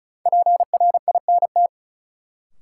Morse code, heard as a keyed tone, about 36 wpm, 700 Hz. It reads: PRINT
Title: Sentences from Top 1400 Words - New Word Drill - 35wpm